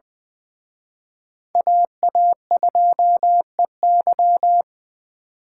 Morse code, heard as a keyed tone, about 24 wpm, 700 Hz. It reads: AA2EY
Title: Speed Racing - USA 2x2 Call Signs 20wpm